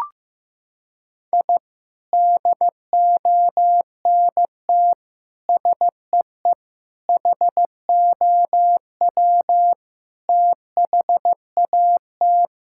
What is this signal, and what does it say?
Morse code, heard as a keyed tone, about 15 wpm, 700 Hz. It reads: I DONT SEE HOW THAT
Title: Sentences from Top 1000 Words - Review Part 6 - 15wpm